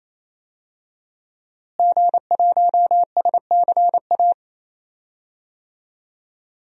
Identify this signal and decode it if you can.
Morse code, heard as a keyed tone, about 28 wpm, 700 Hz. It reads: G1SCA
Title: DXCC-100 Call Signs - 5 Characters - Repeated 3x - 28wpm